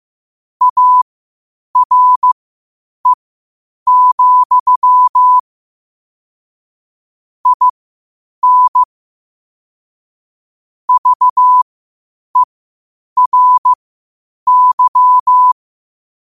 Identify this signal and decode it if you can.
Morse code, heard as a keyed tone, about 15 wpm, spaced as for 8 wpm, 1000 Hz. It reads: ARE, IN VERY
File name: qotdfiles / 8wpm